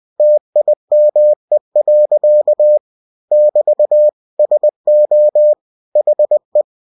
Morse code, heard as a keyed tone, about 20 wpm, 600 Hz. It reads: TIME. <BT> SO HE
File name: cw-books / winnie-the-pooh / winnie-20wpm-page-0025